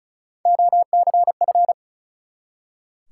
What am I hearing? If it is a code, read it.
Morse code, heard as a keyed tone, about 36 wpm, 700 Hz. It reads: OCF
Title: Top 180 QSO Elements 35wpm